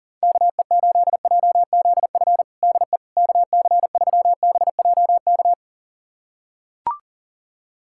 Morse code, heard as a keyed tone, about 40 wpm, 700 Hz. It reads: KE8JZF DE KC3BJK
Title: USA Callsign DE Callsign - 6 Characters - 40wpm